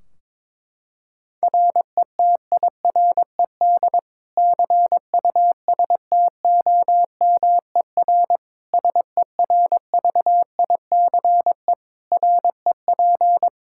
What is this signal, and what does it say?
Morse code, heard as a keyed tone, about 22 wpm, 700 Hz. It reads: RETIRED CUSTOMER SERVICE REP
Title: Practice QSO - Exchange 3 - More Info - 22wpm